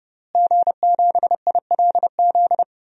Morse code, heard as a keyed tone, about 32 wpm, 700 Hz. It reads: G7ILZ
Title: DXCC-100 Call Signs - 5 Characters - Repeated 3x - 30wpm